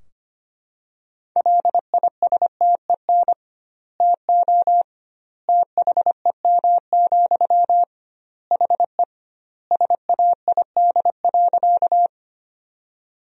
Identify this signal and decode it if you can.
Morse code, heard as a keyed tone, about 25 wpm, 700 Hz. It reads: LISTEN TO THEM, HE SAID.
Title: 2x Word-Spacing - Sentences from Top 400 Words - 25wpm